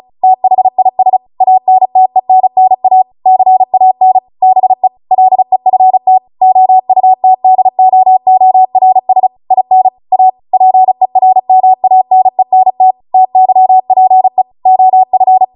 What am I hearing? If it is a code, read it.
Morse code, heard as a keyed tone, about 36 wpm, 750 Hz. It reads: THIS ANTENNA CAN BE LEFT OUTDOORS IN A PERMANENT TYPE OF